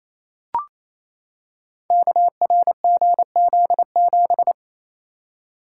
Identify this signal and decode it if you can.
Morse code, heard as a keyed tone, about 29 wpm, 700 Hz. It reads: KRGZ7
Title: Code Groups - 5 Letter:Numbers - 28wpm